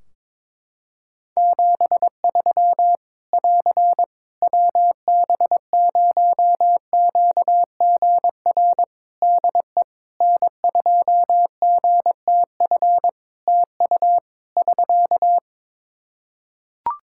Morse code, heard as a keyed tone, about 22 wpm, 700 Hz. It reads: 73 <AR> WB0QGR DE N2GTF TU <SK>
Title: Practice QSO - Exchange 4 - Ending - 22wpm